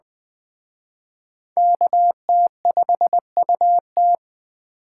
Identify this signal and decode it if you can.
Morse code, heard as a keyed tone, about 21 wpm, 700 Hz. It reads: KT5UT
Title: Speed Racing - USA 2x2 Call Signs 17wpm